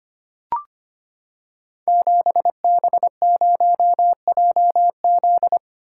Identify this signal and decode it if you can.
Morse code, heard as a keyed tone, about 25 wpm, 700 Hz. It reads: 7B0JZ
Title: Code Groups - 5 Letter:Numbers - 25wpm